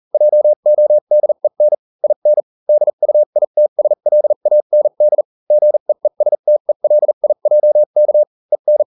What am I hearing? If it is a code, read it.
Morse code, heard as a keyed tone, about 40 wpm, spaced as for 35 wpm, 600 Hz. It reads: JODEN IN DUITSLAND GEESTELIJK EN